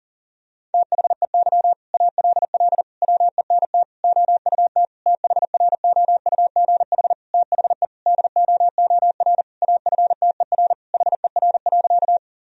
Morse code, heard as a keyed tone, about 40 wpm, 700 Hz. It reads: THEY ALL WENT OUT THROUGH THE DOOR AFTER HER.
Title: Sentences from Top 1000 Words - Review Part 1 - 40wpm